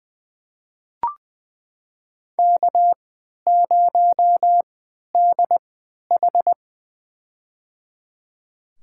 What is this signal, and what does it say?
Morse code, heard as a keyed tone, about 20 wpm, spaced as for 11 wpm, 700 Hz. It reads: K0DH
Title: USA Call Signs - 3x Character Spacing - 20wpm